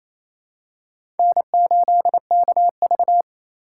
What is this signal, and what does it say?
Morse code, heard as a keyed tone, about 28 wpm, 700 Hz. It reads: N8KV
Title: USA Call Signs - 4 Characters - 2x Character Spacing - 28wpm